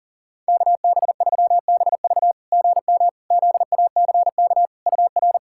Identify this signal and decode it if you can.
Morse code, heard as a keyed tone, about 40 wpm, 690 Hz. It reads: KB3BV GM ZACK UR
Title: SOTA QSO Practice 40wpm